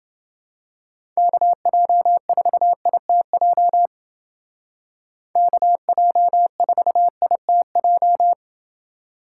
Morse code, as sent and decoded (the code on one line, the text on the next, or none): -.- .--- ....- .. - .--- / -.- .--- ....- .. - .---
KJ4ITJ KJ4ITJ